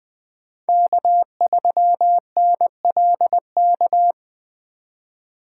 Morse code, heard as a keyed tone, about 20 wpm, 700 Hz. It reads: K3NLK
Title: USA Call Signs - 5 Characters - 3x Character Spacing - 20wpm